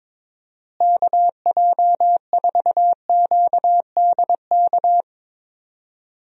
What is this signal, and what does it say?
Morse code, heard as a keyed tone, about 22 wpm, 700 Hz. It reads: KJ4QDK
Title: USA Call Signs - 6 Characters - Repeated 3x - 22wpm